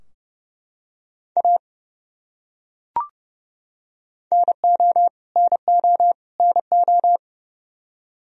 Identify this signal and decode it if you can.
Morse code, heard as a keyed tone, about 32 wpm, 700 Hz. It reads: A NO NO NO
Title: Top 100 Words Repeated 30wpm